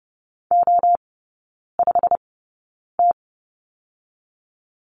Morse code, heard as a keyed tone, about 30 wpm, spaced as for 9 wpm, 700 Hz. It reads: O5T
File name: Lesson 17e - Letter u - Three Characters 30wpm